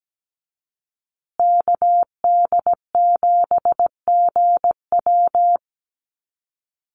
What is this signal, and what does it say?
Morse code, heard as a keyed tone, about 17 wpm, 700 Hz. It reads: KD7GW
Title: USA Call Signs Repeated 2x 17wpm